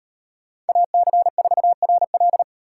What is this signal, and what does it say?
Morse code, heard as a keyed tone, about 38 wpm, 700 Hz. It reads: AC4RL